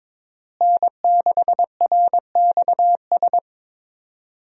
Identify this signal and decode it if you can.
Morse code, heard as a keyed tone, about 22 wpm, 700 Hz. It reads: N6RXS
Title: USA Call Signs - 5 Characters - Repeated 3x - 22wpm